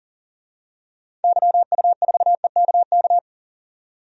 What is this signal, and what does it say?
Morse code, heard as a keyed tone, about 38 wpm, 700 Hz. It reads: YU4EKK
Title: DXCC-100 Call Signs - 6 Characters - 40wpm